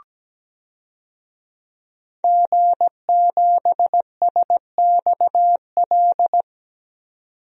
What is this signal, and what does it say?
Morse code, heard as a keyed tone, about 17 wpm, 700 Hz. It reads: G7SXL
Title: DXCC-100 Call Signs - 5 Characters - 17wpm